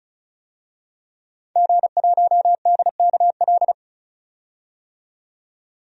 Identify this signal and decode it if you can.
Morse code, heard as a keyed tone, about 36 wpm, 700 Hz. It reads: G1DKL
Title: DXCC-100 Call Signs - 5 Characters - 35wpm